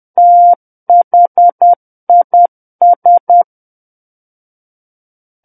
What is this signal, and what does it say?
Morse code, heard as a keyed tone, about 10 wpm, 700 Hz. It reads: THIS
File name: Book One / 01-Chapter 1 - 10 WPM